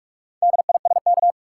Morse code, heard as a keyed tone, about 46 wpm, 700 Hz. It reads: DISK
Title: Sentences from Top 1000 Words - Review Part 6 - New Word Drill - 45wpm